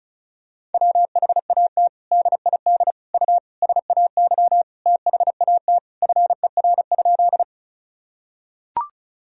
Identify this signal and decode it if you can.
Morse code, heard as a keyed tone, about 36 wpm, 700 Hz. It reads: WHAT DID U SAY THAT FER?
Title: Sentences from Top 200 Words and Abbreviations 35wpm